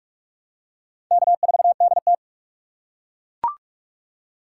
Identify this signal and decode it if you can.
Morse code, heard as a keyed tone, about 40 wpm, 700 Hz. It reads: K4DT